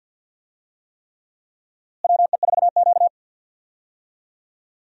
Morse code, heard as a keyed tone, about 46 wpm, 700 Hz. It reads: WE4Q
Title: USA Call Signs - 4 Characters - Repeated 3x - 50wpm